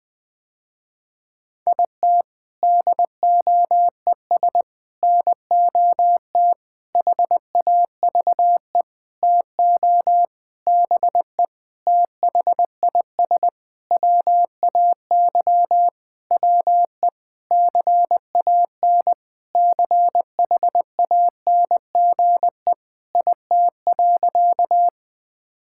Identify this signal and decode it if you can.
Morse code, heard as a keyed tone, about 20 wpm, 700 Hz. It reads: IT DOES NOT HAVE TO BE THIS WAY WE CAN CHANGE IT.
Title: Sentences with 7-Word Phrases - Spoken Beginning - 20wpm